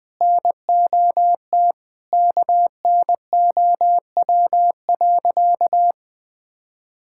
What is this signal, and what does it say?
Morse code, heard as a keyed tone, about 20 wpm, 700 Hz. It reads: NOT KNOW.
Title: Sentences with 4-Word Phrases - 20wpm